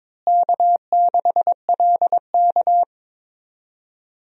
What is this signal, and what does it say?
Morse code, heard as a keyed tone, about 22 wpm, 700 Hz. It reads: K6LK
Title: USA Call Signs - 4 Characters - Repeated 3x - 22wpm